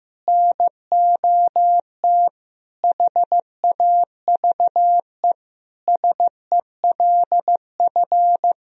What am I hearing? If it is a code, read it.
Morse code, heard as a keyed tone, about 15 wpm, 700 Hz. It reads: NOT HAVE SELF
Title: Sentences from Top 1000 Words - Review Part 3 - 15wpm